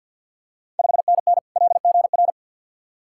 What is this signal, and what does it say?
Morse code, heard as a keyed tone, about 50 wpm, 690 Hz. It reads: 5NN LGR
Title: Ontario QSO Party Exchange 50wpm